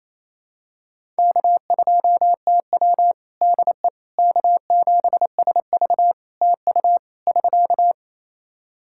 Morse code, heard as a keyed tone, about 29 wpm, 700 Hz. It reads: K2TW DE K7SV TU <SK>
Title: K1USN SST QSO Practice 28wpm